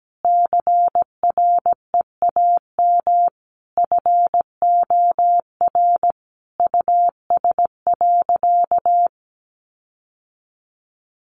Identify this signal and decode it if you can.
Morse code, heard as a keyed tone, about 17 wpm, 700 Hz. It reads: CREAM FOR US.
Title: Sentences from Top 1500 Words 17wpm